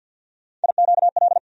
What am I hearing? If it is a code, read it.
Morse code, heard as a keyed tone, about 52 wpm, 700 Hz. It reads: IQL
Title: USA Call Sign Postfixes - 3 Letters 50wpm